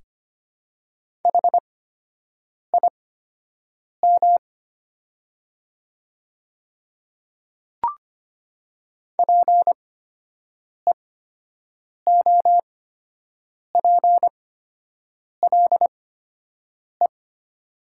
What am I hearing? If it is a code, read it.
Morse code, heard as a keyed tone, about 25 wpm, spaced as for 7 wpm, 700 Hz. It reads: HIM PEOPLE
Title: 8x Character Spacing - Top 100 Words - 25wpm